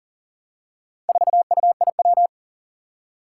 Morse code, heard as a keyed tone, about 37 wpm, 700 Hz. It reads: 4UIW
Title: Four Challenging Letter-Number Combinations 40wpm